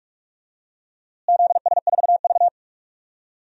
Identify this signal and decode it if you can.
Morse code, heard as a keyed tone, about 46 wpm, 700 Hz. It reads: ZS4V
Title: DXCC-100 Call Signs - 4 Characters - Repeated 3x - 45wpm